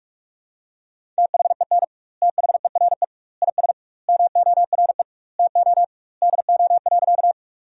Morse code, heard as a keyed tone, about 46 wpm, 700 Hz. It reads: THEN THERE IS MORE TO DO.
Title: 2x Word-Spacing - Sentences from Top 100 Words - 45wpm